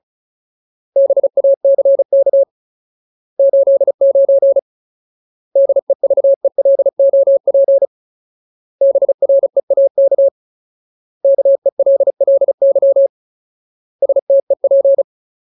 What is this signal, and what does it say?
Morse code, heard as a keyed tone, about 34 wpm, 550 Hz. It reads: BACK 89 DEVELOP BREAK KELLY STEP
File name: Ultimate Word List / 35WPM Ultimate